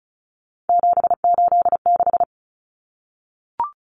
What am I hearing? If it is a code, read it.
Morse code, heard as a keyed tone, about 34 wpm, 700 Hz. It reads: 786
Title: Common Cut Numbers - 3-Digit 35wpm